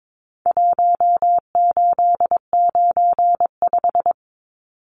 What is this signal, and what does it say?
Morse code, heard as a keyed tone, about 22 wpm, 700 Hz. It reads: 1895
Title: Numbers - Contest Style - 4-Digit 22wpm